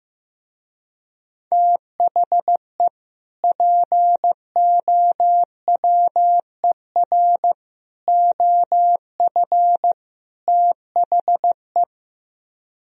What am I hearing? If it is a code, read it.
Morse code, heard as a keyed tone, about 15 wpm, 700 Hz. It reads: THE POWER OF THE
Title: Speed-Racing - Sets of 4 Words 15wpm